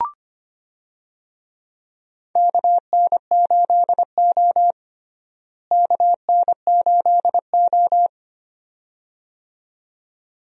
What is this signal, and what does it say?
Morse code, heard as a keyed tone, about 25 wpm, 700 Hz. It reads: KN8O KN8O
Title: USA Call Signs - 4 Characters - Repeated 2x - 25wpm